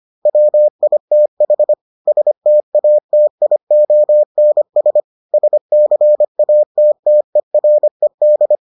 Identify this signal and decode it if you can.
Morse code, heard as a keyed tone, about 25 wpm, 600 Hz. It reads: WITH STATIONS SCATTERED